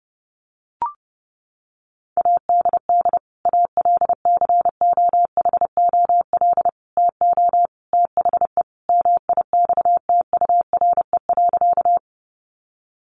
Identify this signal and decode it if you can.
Morse code, heard as a keyed tone, about 28 wpm, 700 Hz. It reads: ADD ALCOHOL TO THE MIXTURE.
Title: Sentences from Top 1500 Words 30wpm